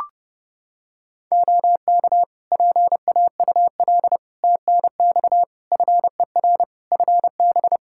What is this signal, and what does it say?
Morse code, heard as a keyed tone, about 30 wpm, 700 Hz. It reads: OK PAUL TNX FER FB
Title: K1USN SST QSO Practice 30wpm